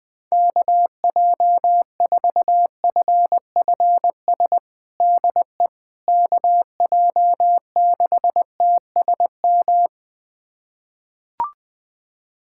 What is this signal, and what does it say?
Morse code, heard as a keyed tone, about 21 wpm, 700 Hz. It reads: KJ4FFS DE KJ6TSM
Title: USA Callsign DE Callsign - 6 Characters - 20wpm